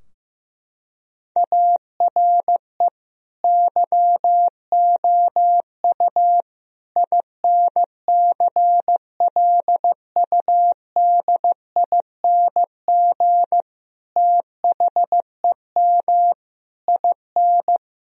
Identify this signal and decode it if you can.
Morse code, heard as a keyed tone, about 15 wpm, 700 Hz. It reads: ARE YOU INCLUDING THEM IN
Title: Sentences from Top 1000 Words - Review Part 4 - 15wpm